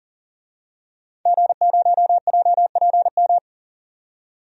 Morse code, heard as a keyed tone, about 40 wpm, 700 Hz. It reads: G0JPM
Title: DXCC-100 Call Signs - 5 Characters - 40wpm